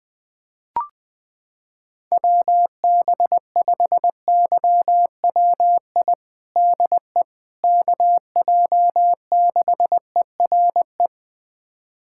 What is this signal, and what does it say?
Morse code, heard as a keyed tone, about 20 wpm, 700 Hz. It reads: WB5YWI DE KJ6ERE
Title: USA Callsign DE Callsign - 6 Characters - 20wpm